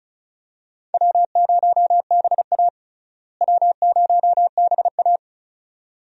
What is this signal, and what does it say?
Morse code, heard as a keyed tone, about 36 wpm, 700 Hz. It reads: W0BA W0BA
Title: USA Call Signs - 4 Characters - Repeated 2x - 35wpm